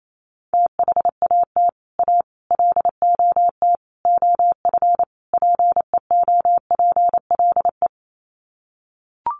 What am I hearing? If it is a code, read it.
Morse code, heard as a keyed tone, about 28 wpm, 700 Hz. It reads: THAT A LOT OF PEOPLE
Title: Sets of 5 Words 28wpm